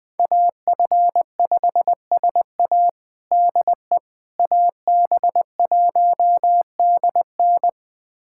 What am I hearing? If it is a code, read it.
Morse code, heard as a keyed tone, about 19 wpm, 700 Hz. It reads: AF5SA DE AB1DN
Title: USA Callsign DE Callsign - 5 Characters - 20wpm